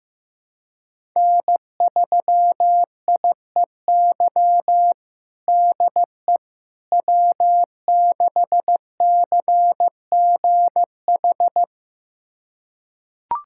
Morse code, heard as a keyed tone, about 15 wpm, 700 Hz. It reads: N3IEY DE W6CGH
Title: USA Callsign DE Callsign - 5 Characters - 15wpm